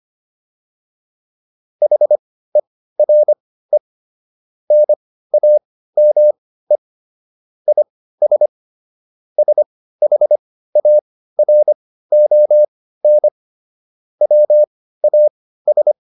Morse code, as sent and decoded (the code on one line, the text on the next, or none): .... .-. / -. .- -- . / .. ... / ... .... .- .-. --- -. / .-- .- ...
HERE NAME IS SHARON WAS